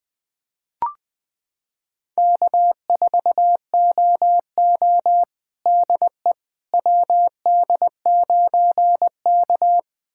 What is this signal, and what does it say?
Morse code, heard as a keyed tone, about 20 wpm, 700 Hz. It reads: K4OO DE WD9K
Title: USA Callsign DE Callsign - 4 Characters - 20wpm